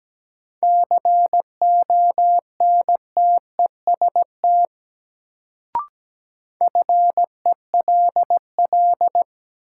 Morse code, heard as a keyed tone, about 17 wpm, 700 Hz. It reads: CONTEST FELL
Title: Sentences from Top 1600 Words - New Word Drill Alt - 17wpm